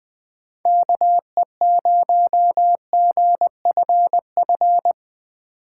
Morse code, heard as a keyed tone, about 20 wpm, 700 Hz. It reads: KE0GFF
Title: USA Call Signs - 6 Characters - 3x Character Spacing - 20wpm